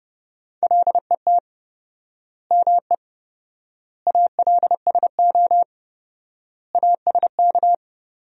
Morse code, heard as a keyed tone, about 30 wpm, 700 Hz. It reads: LET ME ALSO ASK
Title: 4x Word-Spacing - Sentences from Top 400 Words - 30wpm